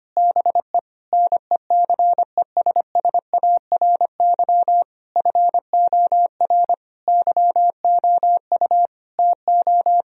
Morse code, heard as a keyed tone, about 25 wpm, 700 Hz. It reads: BE NECESSARY FOR YOU TO
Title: Sentences from Top 1000 Words - Review Part 1 - 25wpm